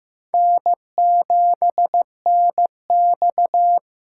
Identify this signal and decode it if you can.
Morse code, heard as a keyed tone, about 15 wpm, 700 Hz. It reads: N7NX
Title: USA Call Signs - 2x Character Spacing - 15wpm